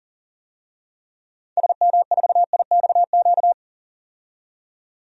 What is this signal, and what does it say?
Morse code, heard as a keyed tone, about 40 wpm, 700 Hz. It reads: SM4IXQ
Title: DXCC-100 Call Signs - 6 Characters - 40wpm